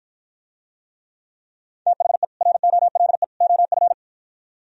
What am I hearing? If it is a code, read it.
Morse code, heard as a keyed tone, about 50 wpm, 700 Hz. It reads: THE ROLE OF